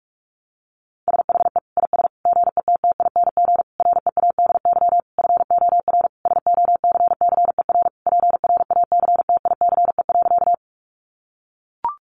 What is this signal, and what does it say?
Morse code, heard as a keyed tone, about 44 wpm, 700 Hz. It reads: SHE IS GETTING READY FOR SOCCER PRACTICE.
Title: Easy Phrases 45wpm